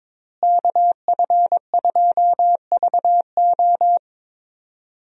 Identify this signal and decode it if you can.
Morse code, heard as a keyed tone, about 22 wpm, 700 Hz. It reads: KF2VO
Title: USA Call Signs - 5 Characters - Repeated 3x - 22wpm